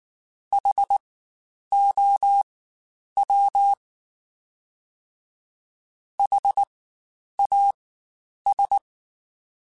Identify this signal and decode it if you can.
Morse code, heard as a keyed tone, about 19 wpm, spaced as for 9 wpm, 800 Hz. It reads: HOW HAS
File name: Spaced Alphas / Spaced Alphas mp3s / Hs sp @18